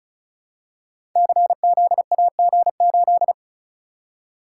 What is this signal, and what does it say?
Morse code, heard as a keyed tone, about 36 wpm, 690 Hz. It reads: CZAG8
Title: Code Groups - 5 Letter:Numbers - 35wpm